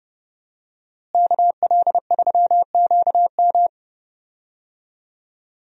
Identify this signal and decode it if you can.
Morse code, heard as a keyed tone, about 30 wpm, 700 Hz. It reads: KL3QM
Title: USA Call Signs - 5 Characters - Repeated 2x - 30wpm